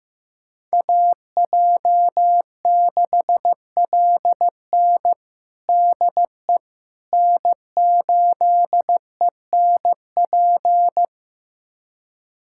Morse code, heard as a keyed tone, about 15 wpm, 700 Hz. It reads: AJ6LN DE N8ENP
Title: USA Callsign DE Callsign - 5 Characters - 15wpm